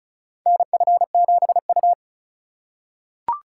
Morse code, heard as a keyed tone, about 34 wpm, 700 Hz. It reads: NF7U
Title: USA Call Signs - 4 Characters - 3x Character Spacing - 35wpm